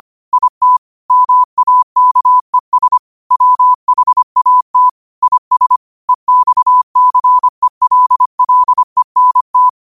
Morse code, heard as a keyed tone, about 25 wpm, 1000 Hz. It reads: IT MAKES WHAT IS EXCELLENT
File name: qotdfiles / 25wpm